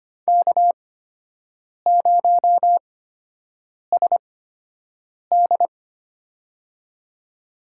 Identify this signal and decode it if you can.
Morse code, heard as a keyed tone, about 25 wpm, spaced as for 7 wpm, 700 Hz. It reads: K0SD
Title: USA Call Signs - 4 Characters - 8x Character Spacing - 25wpm